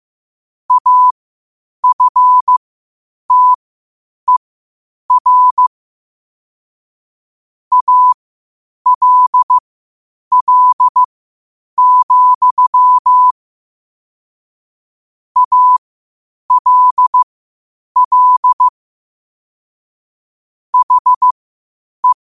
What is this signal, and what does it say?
Morse code, heard as a keyed tone, about 15 wpm, spaced as for 8 wpm, 1000 Hz. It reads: AFTER ALL, ALL HE
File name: qotdfiles / 8wpm